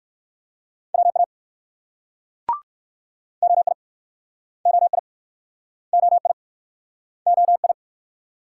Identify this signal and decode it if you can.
Morse code, heard as a keyed tone, about 60 wpm, 700 Hz. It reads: WA OI OI OI OI